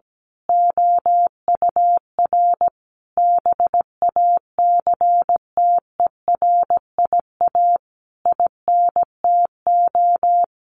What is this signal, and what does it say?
Morse code, heard as a keyed tone, about 17 wpm, 700 Hz. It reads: OUR BACTERIA INTO
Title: A Sound Of Thunder 17wpm